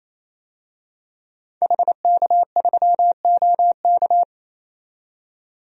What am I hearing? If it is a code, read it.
Morse code, heard as a keyed tone, about 28 wpm, 700 Hz. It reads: HK3OK
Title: DXCC-100 Call Signs - 5 Characters - 28wpm